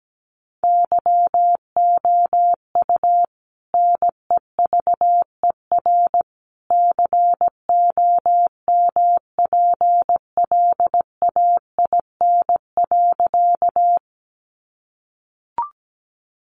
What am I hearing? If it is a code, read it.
Morse code, heard as a keyed tone, about 17 wpm, 700 Hz. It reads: YOU NEVER COMPLAIN.
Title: Sentences from Top 1300 Words 17wpm